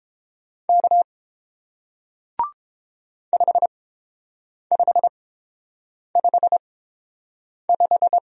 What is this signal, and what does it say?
Morse code, heard as a keyed tone, about 30 wpm, spaced as for 7 wpm, 700 Hz. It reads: K 5555